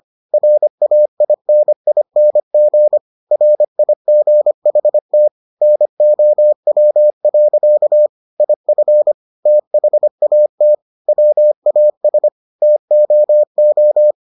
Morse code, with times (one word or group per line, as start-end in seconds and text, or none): 0.3-3.0 RAINING
3.3-5.3 RIGHT
5.6-8.1 NOW.
8.4-9.1 IF
9.5-10.7 THAT
11.1-12.3 WAS
12.6-14.1 TOO